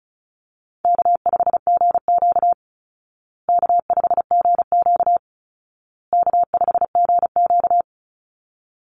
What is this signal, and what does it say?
Morse code, heard as a keyed tone, about 34 wpm, 700 Hz. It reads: K5GQ K5GQ K5GQ
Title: Easy Call Signs Repeated 3x - 35wpm